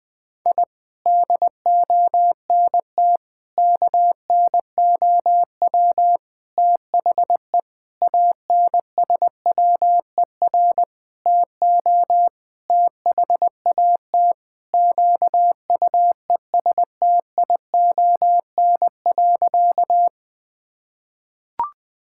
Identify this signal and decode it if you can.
Morse code, heard as a keyed tone, about 21 wpm, 700 Hz. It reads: I DONT KNOW THE ANSWER TO THAT QUESTION.
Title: Sentences with 7-Word Phrases - 20wpm